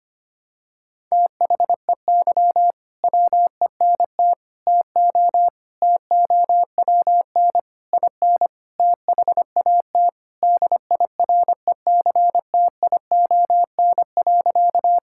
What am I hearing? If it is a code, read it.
Morse code, heard as a keyed tone, about 25 wpm, 700 Hz. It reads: THEY WENT TO TOWN IN THAT DIRECTION.
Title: Sentences from Top 1000 Words - Review Part 5 - 25wpm